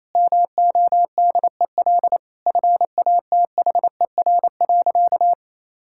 Morse code, heard as a keyed tone, about 28 wpm, 700 Hz. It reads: MODEL FATHER.
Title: Sentences from Top 1000 Words - Review Part 5 - 28wpm